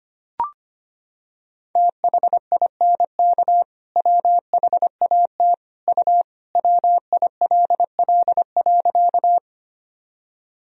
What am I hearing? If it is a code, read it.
Morse code, heard as a keyed tone, about 25 wpm, 700 Hz. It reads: THINK WHAT U WILL.
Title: Sentences from Top 200 Words and Abbreviations 25wpm